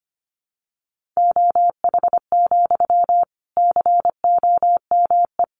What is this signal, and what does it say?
Morse code, heard as a keyed tone, about 25 wpm, 700 Hz. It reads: OH, COME
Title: Sentences from Top 500 Words 25wpm